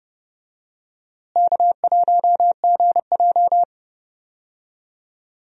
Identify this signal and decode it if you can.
Morse code, heard as a keyed tone, about 30 wpm, 700 Hz. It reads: K1GJ
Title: USA Call Signs - 4 Characters - Repeated 2x - 30wpm